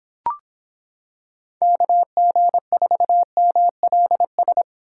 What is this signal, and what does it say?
Morse code, heard as a keyed tone, about 26 wpm, 700 Hz. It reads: KG4MLS